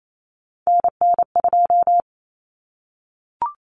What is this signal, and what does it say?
Morse code, heard as a keyed tone, about 28 wpm, 700 Hz. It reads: NN2
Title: Common Cut Numbers - 3-Digit 28wpm